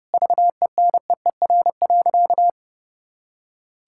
Morse code, heard as a keyed tone, about 30 wpm, 700 Hz. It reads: VENEER.
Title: Pangram Challenge v2 - 30wpm